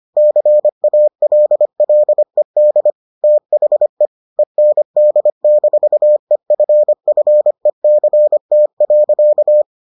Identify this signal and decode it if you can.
Morse code, heard as a keyed tone, about 25 wpm, 600 Hz. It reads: CALLED THE END-EFFECT.